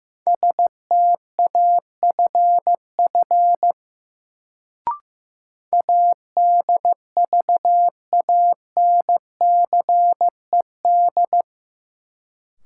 Morse code, heard as a keyed tone, about 15 wpm, 700 Hz. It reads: STAFF ADVANCED
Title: Sentences from Top 1200 Words - New Word Drill - 15wpm